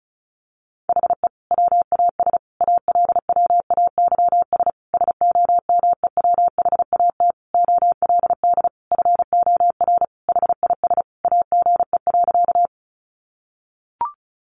Morse code, heard as a keyed tone, about 34 wpm, 700 Hz. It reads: HE WAS ALWAYS SOMEWHAT OLD FOR HIS AGE.
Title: Sentences from Top 1300 Words 35wpm